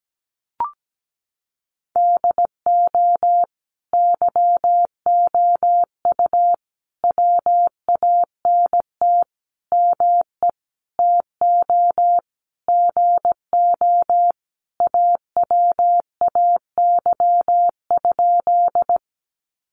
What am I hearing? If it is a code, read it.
Morse code, heard as a keyed tone, about 17 wpm, 700 Hz. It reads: DO YOU WANT ME TO GO AWAY?
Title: Sentences from Top 600 Words 17wpm